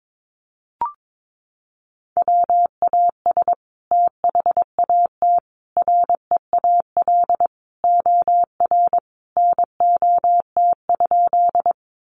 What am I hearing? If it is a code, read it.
Morse code, heard as a keyed tone, about 22 wpm, 700 Hz. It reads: WAS THAT REAL OR NOT?
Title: Sentences from Top 600 Words 22wpm